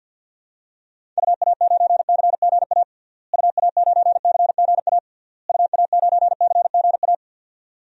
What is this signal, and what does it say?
Morse code, heard as a keyed tone, about 51 wpm, 700 Hz. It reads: UA9CGA UA9CGA UA9CGA
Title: DXCC-100 Call Signs - 6 Characters - Repeated 3x - 50wpm